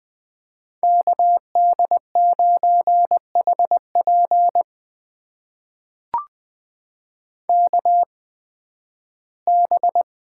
Morse code, heard as a keyed tone, about 20 wpm, spaced as for 13 wpm, 700 Hz. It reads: KD9HP KB
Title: USA Call Signs - 8x Character Spacing - 20wpm